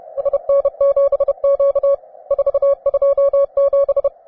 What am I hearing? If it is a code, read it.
Morse code, heard as a keyed tone, about 31 wpm, 560 Hz. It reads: SN7Q 427